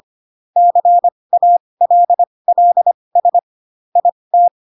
Morse code, heard as a keyed tone, about 25 wpm, spaced as for 20 wpm, 700 Hz. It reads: CALLS IT
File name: Pinocchio / Pinocchio ch3 15 25 ik1hge